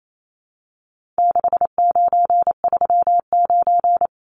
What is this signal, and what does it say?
Morse code, heard as a keyed tone, about 28 wpm, 700 Hz. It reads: 6939
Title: Numbers - 4-Digit 28wpm